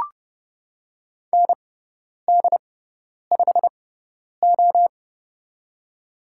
Morse code, heard as a keyed tone, about 30 wpm, spaced as for 10 wpm, 690 Hz. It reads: ND5O
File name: Lesson 18g - Letter m - Call Signs 30wpm